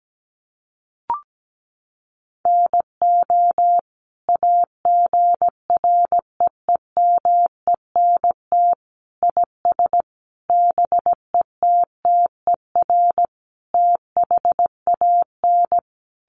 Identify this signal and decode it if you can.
Morse code, heard as a keyed tone, about 17 wpm, 700 Hz. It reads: NO AGREEMENT IS BETTER THAN
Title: Sentences from Top 1500 Words 17wpm